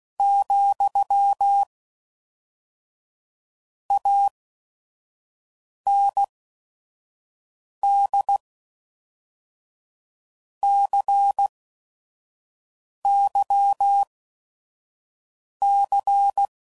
Morse code, heard as a keyed tone, about 16 wpm, 800 Hz. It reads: , A N D C Y C